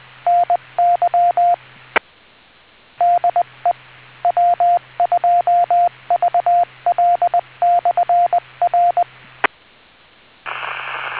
Morse code, heard as a keyed tone, about 20 wpm, 700 Hz, with radio noise. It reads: NY DE W2VL/R